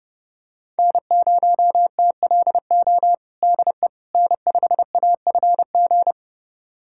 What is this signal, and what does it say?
Morse code, heard as a keyed tone, about 32 wpm, 700 Hz. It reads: N0TLO DE N5AFG
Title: USA Callsign DE Callsign - 5 Characters - 30wpm